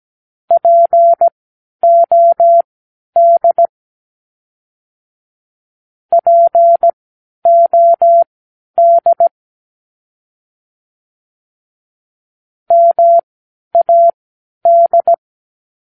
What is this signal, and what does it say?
Morse code, heard as a keyed tone, about 17 wpm, spaced as for 10 wpm, 680 Hz. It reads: POD POD MAD
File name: k7qo / Practicewords